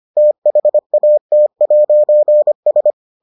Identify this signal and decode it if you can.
Morse code, heard as a keyed tone, about 25 wpm, 600 Hz. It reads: THAT'S